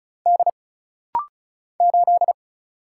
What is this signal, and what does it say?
Morse code, heard as a keyed tone, about 36 wpm, 700 Hz. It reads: D 8
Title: Single Letter-Number Alt - Rapid-Fire 35wpm